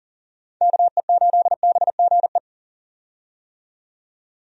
Morse code, heard as a keyed tone, about 40 wpm, 700 Hz. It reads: KE8BGE
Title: USA Call Signs - 6 Characters - 40wpm